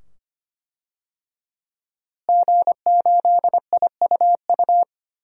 Morse code, heard as a keyed tone, about 25 wpm, 700 Hz. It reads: G8IUU
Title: DXCC-100 Call Signs - 5 Characters - Repeated 2x - 25wpm